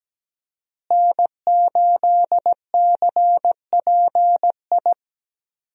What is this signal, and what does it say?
Morse code, heard as a keyed tone, about 17 wpm, 700 Hz. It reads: N8CPI
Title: USA Call Signs - 5 Characters - Repeated 2x - 17wpm